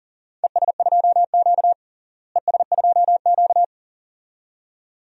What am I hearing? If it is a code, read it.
Morse code, heard as a keyed tone, about 40 wpm, 700 Hz. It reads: ES2Q ES2Q
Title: DXCC-100 Call Signs - 4 Characters - Repeated 2x - 40wpm